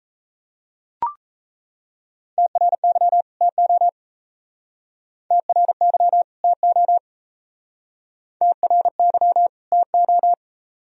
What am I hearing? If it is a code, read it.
Morse code, heard as a keyed tone, about 34 wpm, 700 Hz. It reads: TRY TO TRY TO TRY TO
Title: Speed-Racing - Sets of 2 Words 28wpm